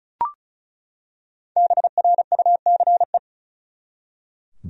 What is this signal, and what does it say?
Morse code, heard as a keyed tone, about 35 wpm, 700 Hz. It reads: BRUCE
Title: USA Common Names - 35wpm